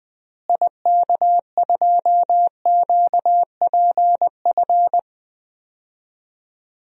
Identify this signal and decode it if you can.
Morse code, heard as a keyed tone, about 20 wpm, 700 Hz. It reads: IK2QPF
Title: DXCC-100 Call Signs - 6 Characters - Repeated 2x - 20wpm